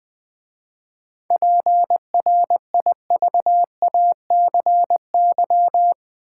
Morse code, heard as a keyed tone, about 20 wpm, 700 Hz. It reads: PRIVACY
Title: 7 Letter Words - First 3 Letters Given - 20wpm